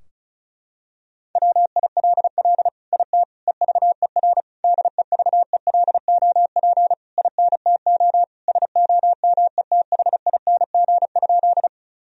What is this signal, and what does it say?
Morse code, heard as a keyed tone, about 36 wpm, 700 Hz. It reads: WILL IT EVER DEVELOP INTO SOMETHING?
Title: Sentences from Top 1000 Words - Review Part 2 - 35wpm